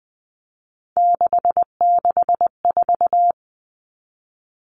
Morse code, heard as a keyed tone, about 20 wpm, 700 Hz. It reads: 664
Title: Common Cut Numbers - 3-Digit 20wpm